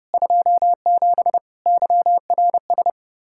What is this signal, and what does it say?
Morse code, heard as a keyed tone, about 30 wpm, 700 Hz. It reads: 27 YRS
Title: Practice QSO - Exchange 3 - More Info - 30wpm